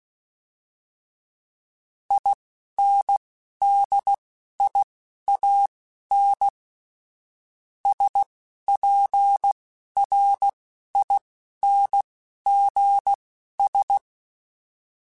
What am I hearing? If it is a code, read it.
Morse code, heard as a keyed tone, about 16 wpm, spaced as for 11 wpm, 800 Hz. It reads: INDIAN SPRINGS
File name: fromSMs / FromSMs mp3s / Cities@10